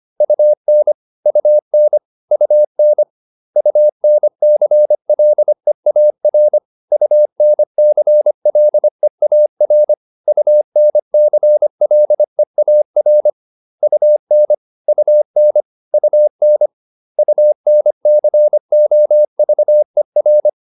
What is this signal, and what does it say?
Morse code, heard as a keyed tone, about 25 wpm, 600 Hz. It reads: UN UN UN UNCLEAR UNCLEAR UNCLEAR UN UN UN UNCOVER